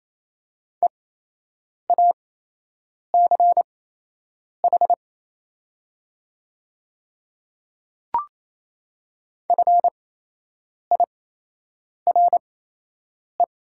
Morse code, heard as a keyed tone, about 28 wpm, spaced as for 8 wpm, 700 Hz. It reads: EACH FIRE
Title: 8x Character Spacing - Top 500 Words - 28wpm